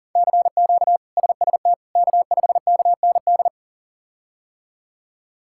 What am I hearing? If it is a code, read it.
Morse code, heard as a keyed tone, about 40 wpm, 690 Hz. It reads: CQ SST K5KND
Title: K1USN SST QSO Practice - 40wpm